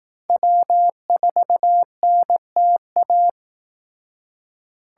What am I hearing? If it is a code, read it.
Morse code, heard as a keyed tone, about 18 wpm, 700 Hz. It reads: W4NTA
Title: Speed Racing - USA Call Signs 15wpm